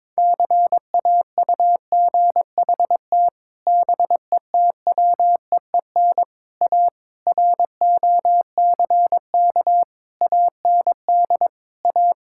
Morse code, heard as a keyed tone, about 22 wpm, 700 Hz. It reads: CAUGHT BETWEEN A ROCK AND A
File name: practice / 22wpm / Sets of 8 Words - 22wpm